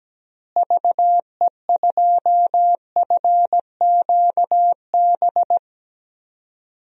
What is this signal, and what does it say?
Morse code, heard as a keyed tone, about 17 wpm, 700 Hz. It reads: VE2FQB
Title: DXCC-100 Call Signs - 6 Characters - Repeated 2x - 17wpm